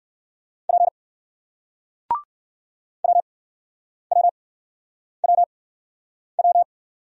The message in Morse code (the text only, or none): P WWWW